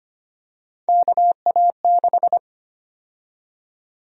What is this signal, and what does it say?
Morse code, heard as a keyed tone, about 25 wpm, 700 Hz. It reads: KA6
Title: USA Call Sign Prefixes - 25wpm